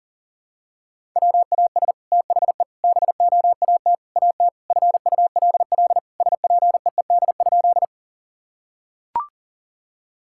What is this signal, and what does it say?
Morse code, heard as a keyed tone, about 37 wpm, 700 Hz. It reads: WAS THE BOAT AT FULL SPEED?
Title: Sentences from Top 1000 Words - Review Part 5 - 40wpm